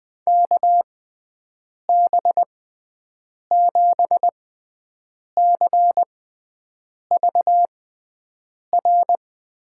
Morse code, heard as a keyed tone, about 20 wpm, spaced as for 7 wpm, 700 Hz. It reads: KB7CVR
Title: USA Call Signs - 6 Characters - 6x Character Spacing - 20wpm